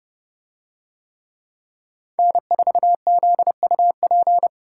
Morse code, heard as a keyed tone, about 29 wpm, 700 Hz. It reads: N4ZUP